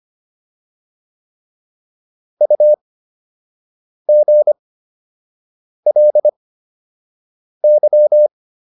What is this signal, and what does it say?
Morse code, heard as a keyed tone, about 25 wpm, spaced as for 6 wpm, 600 Hz. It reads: UGLY